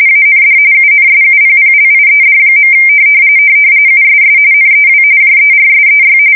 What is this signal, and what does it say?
RTTY, about 45.45 baud. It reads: AND UNSUPPLIED. USS NEW JERSEY
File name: fsk170 / battleshipnj